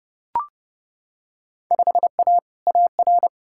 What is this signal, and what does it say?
Morse code, heard as a keyed tone, about 31 wpm, 700 Hz. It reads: 5A AR
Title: Field Day Exchanges Encore - 30wpm